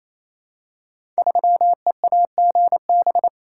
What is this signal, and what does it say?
Morse code, heard as a keyed tone, about 27 wpm, 700 Hz. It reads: 3EAGB
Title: Code Groups - 5 Letter:Numbers - 28wpm